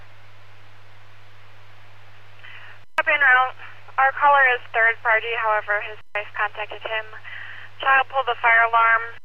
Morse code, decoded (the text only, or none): none